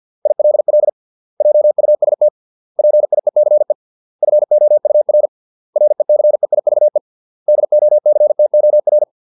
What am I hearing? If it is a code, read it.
Morse code, heard as a keyed tone, about 50 wpm, 600 Hz. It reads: ILL JUST PIECE FOUR RECEIVE DOCTOR